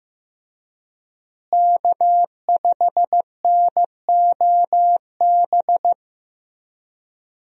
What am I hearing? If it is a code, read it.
Morse code, heard as a keyed tone, about 15 wpm, 700 Hz. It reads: K5NOB
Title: USA Call Signs - 5 Characters - Repeated 3x - 15wpm